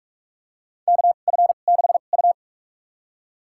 Morse code, heard as a keyed tone, about 45 wpm, spaced as for 32 wpm, 700 Hz. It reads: KF6U
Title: USA Call Signs - 4 Characters - 2x Character Spacing - 45wpm